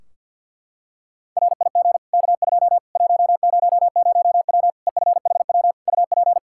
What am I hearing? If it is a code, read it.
Morse code, heard as a keyed tone, about 51 wpm, 700 Hz. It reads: RIG K2 100W EFHW UP
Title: Practice QSO - Exchange 2 - Info - 50wpm